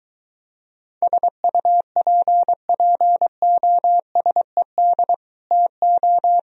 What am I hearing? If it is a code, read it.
Morse code, heard as a keyed tone, about 23 wpm, 700 Hz. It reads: SUPPOSED TO